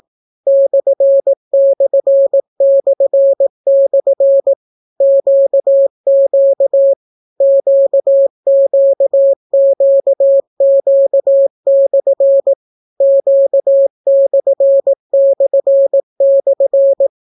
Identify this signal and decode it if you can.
Morse code, heard as a keyed tone, about 18 wpm, 550 Hz. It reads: //// QQ QQQQ/ Q///